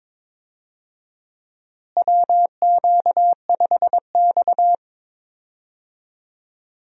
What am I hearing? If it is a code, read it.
Morse code, heard as a keyed tone, about 22 wpm, 700 Hz. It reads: WQ5X
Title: USA Call Signs - 4 Characters - Repeated 2x - 22wpm